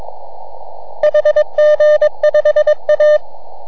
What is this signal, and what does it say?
Morse code, heard as a keyed tone, about 22 wpm, 3000 Hz. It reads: HG5A